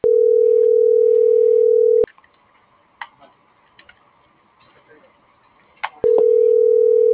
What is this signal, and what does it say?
Ringback tone (North America), 440 + 480 Hz.